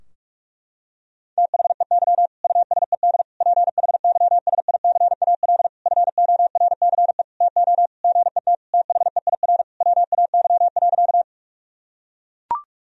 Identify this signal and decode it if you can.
Morse code, heard as a keyed tone, about 44 wpm, 700 Hz. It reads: THEY USED PHYSICAL FORCE TO GET THEIR WAY.
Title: Sentences from Top 1000 Words - Review Part 4 - 45wpm